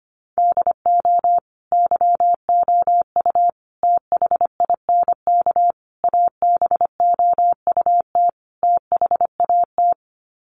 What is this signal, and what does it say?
Morse code, heard as a keyed tone, about 25 wpm, 700 Hz. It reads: DO YOU THINK ABOUT THAT
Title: Sets of 5 Words Encore2 25wpm